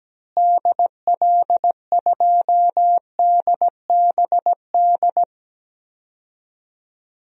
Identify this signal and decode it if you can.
Morse code, heard as a keyed tone, about 17 wpm, 700 Hz. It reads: DL2DBD
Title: DXCC-100 Call Signs - 6 Characters - Repeated 2x - 17wpm